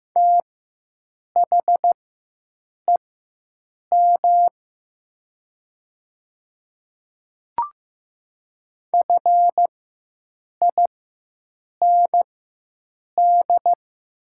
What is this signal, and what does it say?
Morse code, heard as a keyed tone, about 15 wpm, spaced as for 7 wpm, 700 Hz. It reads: THEM FIND
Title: 4x Character Spacing - Top 200 Words - 15wpm